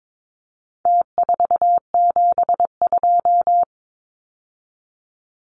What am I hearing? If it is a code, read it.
Morse code, heard as a keyed tone, about 22 wpm, 700 Hz. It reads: T472
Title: Numbers - Contest Style - 4-Digit 22wpm